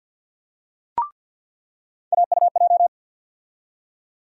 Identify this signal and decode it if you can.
Morse code, heard as a keyed tone, about 52 wpm, 700 Hz. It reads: AUJ
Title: USA Call Sign Postfixes - 3 Letters 50wpm